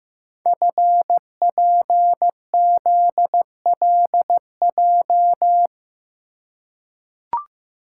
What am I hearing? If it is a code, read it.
Morse code, heard as a keyed tone, about 15 wpm, 700 Hz. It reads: FPZLJ
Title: Code Groups - 5 Letter:Numbers - 15wpm